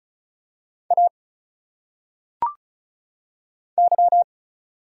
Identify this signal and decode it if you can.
Morse code, heard as a keyed tone, about 35 wpm, 700 Hz. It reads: A Y
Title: Single Letters - Phonetic Pronunciation 35wpm